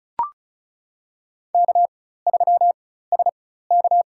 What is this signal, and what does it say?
Morse code, heard as a keyed tone, about 36 wpm, spaced as for 16 wpm, 700 Hz. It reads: K3SK
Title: USA Call Signs - 4 Characters - 4x Character Spacing - 35wpm